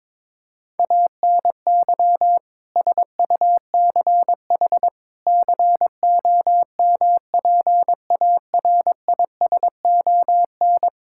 Morse code, heard as a keyed tone, about 22 wpm, 700 Hz. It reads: ANY SUCH COMPARISON
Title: Sentences from Top 1600 Words 22wpm